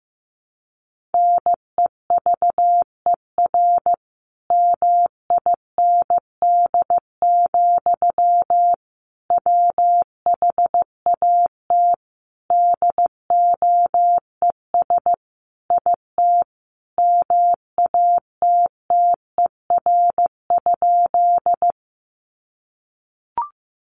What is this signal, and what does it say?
Morse code, heard as a keyed tone, about 15 wpm, 700 Hz. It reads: NEVER MIND, WHAT DOES IT MATTER?
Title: Sentences from Top 700 Words 15wpm